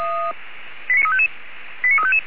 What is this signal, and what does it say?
Five-tone selective call (ZVEI), heard as quick runs of five tones, tones of about 70 ms. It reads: 89355